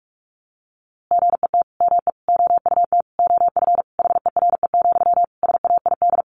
Morse code, heard as a keyed tone, about 46 wpm, 700 Hz. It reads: GET ME OUT OF HERE, SAID
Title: A Sound Of Thunder 45wpm